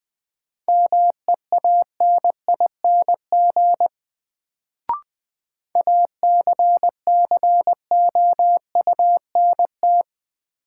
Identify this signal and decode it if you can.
Morse code, heard as a keyed tone, about 20 wpm, 700 Hz. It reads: MEANING ACCOUNT
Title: Sentences from Top 900 Words - New Word Drill - 20wpm